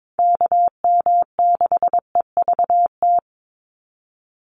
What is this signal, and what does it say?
Morse code, heard as a keyed tone, about 22 wpm, 700 Hz. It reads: KM6EVT
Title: USA Call Signs 22wpm